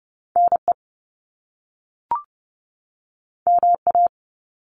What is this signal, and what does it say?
Morse code, heard as a keyed tone, about 30 wpm, 700 Hz. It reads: NE MA
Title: US States 30wpm